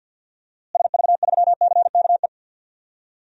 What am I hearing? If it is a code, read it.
Morse code, heard as a keyed tone, about 49 wpm, 700 Hz. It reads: SV3CKE
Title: DXCC-100 Call Signs - 6 Characters - Repeated 3x - 50wpm